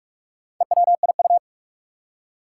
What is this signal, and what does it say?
Morse code, heard as a keyed tone, about 46 wpm, 700 Hz. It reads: EWIU
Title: Four Challenging Letter-Number Combinations 45wpm